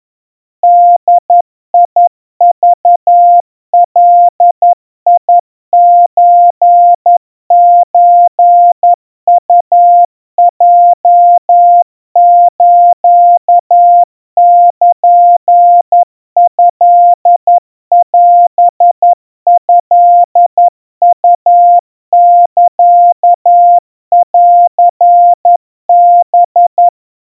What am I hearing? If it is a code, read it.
Morse code (Japanese Wabun code), heard as a keyed tone, about 11 wpm, 700 Hz. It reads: ボクガソソウヲスルトオトウサンハ
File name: kawa / Morse sample / kamyu 40 cpm 700 Hz Crispy